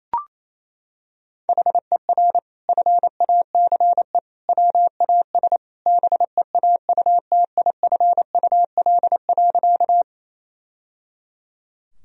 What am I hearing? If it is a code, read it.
Morse code, heard as a keyed tone, about 28 wpm, 700 Hz. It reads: HER FACE WAS BEAUTIFUL.
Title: Sentences from Top 1000 Words - Review Part 2 - 28wpm